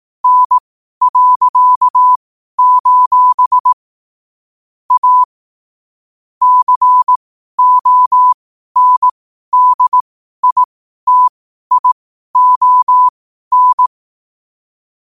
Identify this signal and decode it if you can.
Morse code, heard as a keyed tone, about 18 wpm, spaced as for 12 wpm, 1000 Hz. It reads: N.: A CONDITION
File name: qotdfiles / 12wpm